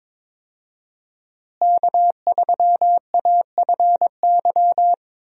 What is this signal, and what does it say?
Morse code, heard as a keyed tone, about 22 wpm, 700 Hz. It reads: K3AFY
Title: USA Call Signs - 5 Characters - Repeated 2x - 22wpm